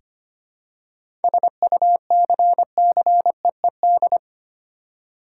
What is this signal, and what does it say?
Morse code, heard as a keyed tone, about 25 wpm, 700 Hz. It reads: SUCCEED
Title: Seven Letter Words - 25wpm